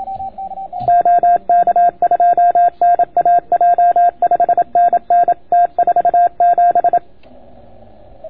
Morse code, heard as a keyed tone, about 27 wpm, 690 Hz. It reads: OK2NAJ 5NN T47